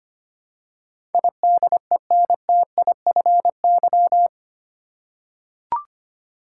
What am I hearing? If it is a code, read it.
Morse code, heard as a keyed tone, about 25 wpm, 700 Hz. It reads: IDENTIFY
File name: Eight Letter Words - 25wpm